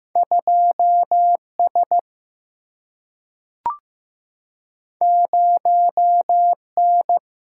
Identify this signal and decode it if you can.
Morse code, heard as a keyed tone, about 15 wpm, 700 Hz. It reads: 2 S 0 N
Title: Single-Letter Single-Number 15wpm